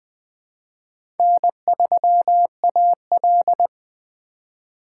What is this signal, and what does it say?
Morse code, heard as a keyed tone, about 20 wpm, 700 Hz. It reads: N3AL
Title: USA Call Signs - 3x Character Spacing - 20wpm